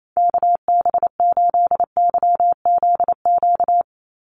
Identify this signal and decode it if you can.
Morse code, heard as a keyed tone, about 28 wpm, 700 Hz. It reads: KB8YZQ
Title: USA Call Signs 28wpm